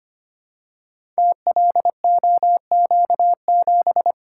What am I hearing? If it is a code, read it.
Morse code, heard as a keyed tone, about 25 wpm, 700 Hz. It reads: TLOQ7
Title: Code Groups - 5 Letter:Numbers - 25wpm